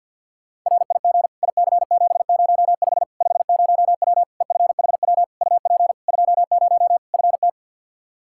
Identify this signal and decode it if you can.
Morse code, heard as a keyed tone, about 52 wpm, 700 Hz. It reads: RIG IC705 50W EFHW UP 20 FT
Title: Practice QSO - Exchange 2 - Info - 50wpm